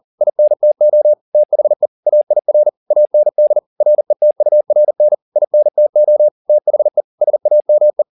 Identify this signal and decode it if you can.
Morse code, heard as a keyed tone, about 40 wpm, 600 Hz. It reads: INTO THE AIR AND RETURN INTO THE SAME
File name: the old man and the sea 2 40wpm